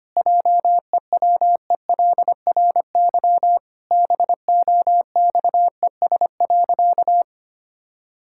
Morse code, heard as a keyed tone, about 25 wpm, 700 Hz. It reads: JEWELRY BOXES.
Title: Pangram Challenge v2 - 25wpm